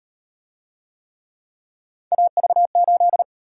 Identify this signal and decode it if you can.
Morse code, heard as a keyed tone, about 37 wpm, 700 Hz. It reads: AV8